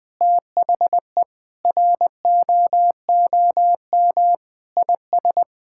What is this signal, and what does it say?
Morse code, heard as a keyed tone, about 20 wpm, 700 Hz. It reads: THE ROOM IS
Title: Sentences with 3-Word Phrases - 20wpm